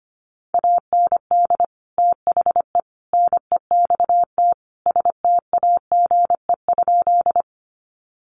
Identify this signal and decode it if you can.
Morse code, heard as a keyed tone, about 25 wpm, 700 Hz. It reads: AND THE NEXT STAGE?
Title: Sentences from Top 1200 Words 25wpm